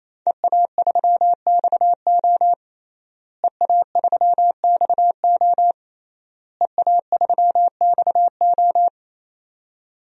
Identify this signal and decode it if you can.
Morse code, heard as a keyed tone, about 28 wpm, 700 Hz. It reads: EA3XO EA3XO EA3XO
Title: DXCC-100 Call Signs - 5 Characters - Repeated 3x - 28wpm